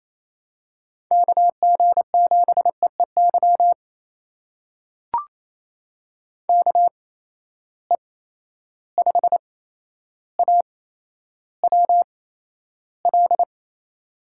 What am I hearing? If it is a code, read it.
Morse code, heard as a keyed tone, about 28 wpm, spaced as for 12 wpm, 700 Hz. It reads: KG7EEY KE5AWL
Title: USA Call Signs - 6 Characters - 8x Character Spacing - 28wpm